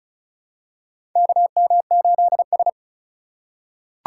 Morse code, heard as a keyed tone, about 34 wpm, 700 Hz. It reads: KM8S
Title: USA Call Signs - 4 Characters - 6x Character Spacing - 35wpm